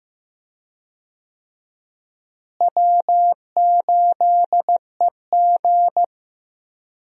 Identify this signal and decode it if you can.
Morse code, heard as a keyed tone, about 15 wpm, 700 Hz. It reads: W8EG
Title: USA Call Signs - 4 Characters - Repeated 3x - 15wpm